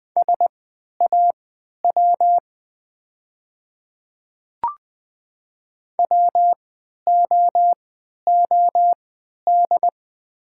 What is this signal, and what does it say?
Morse code, heard as a keyed tone, about 20 wpm, spaced as for 11 wpm, 700 Hz. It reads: SAW WOOD
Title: 3x Character Spacing - Top 300 Words - 20wpm